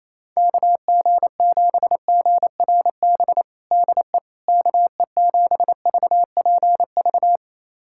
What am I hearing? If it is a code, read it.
Morse code, heard as a keyed tone, about 27 wpm, 700 Hz. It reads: KG7GRB DE KE7VPV
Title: USA Callsign DE Callsign - 6 Characters - 28wpm